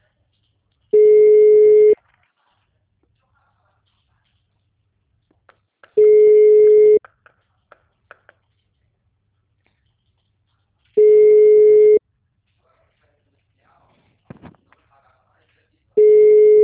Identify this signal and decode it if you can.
Ringback tone (Europe, 425 Hz), with speech.
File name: out- -2011- - - .5891